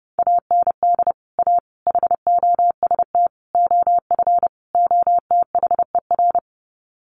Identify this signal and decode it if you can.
Morse code, heard as a keyed tone, about 30 wpm, 700 Hz. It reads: AND A HOST OF OTHER
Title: Sets of 5 Words Encore2 30wpm